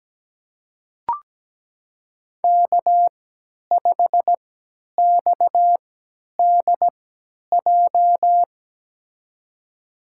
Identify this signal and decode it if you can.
Morse code, heard as a keyed tone, about 17 wpm, spaced as for 10 wpm, 700 Hz. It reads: K5XDJ
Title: USA Call Signs - 5 Characters - 3x Character Spacing - 17wpm